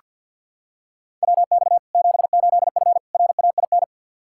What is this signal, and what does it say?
Morse code, heard as a keyed tone, about 46 wpm, 700 Hz. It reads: WX 78F RAIN
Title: K1USN SST QSO Practice 50wpm